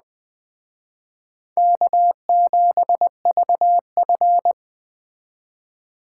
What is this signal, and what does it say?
Morse code, heard as a keyed tone, about 20 wpm, 700 Hz. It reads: K7VF
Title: Speed Racing - USA 1x2 Call Signs 17wpm